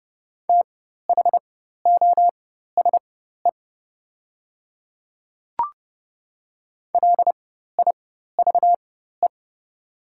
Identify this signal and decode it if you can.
Morse code, heard as a keyed tone, about 31 wpm, spaced as for 14 wpm, 700 Hz. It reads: THOSE LIVE
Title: 4x Character Spacing - Top 300 Words - 30wpm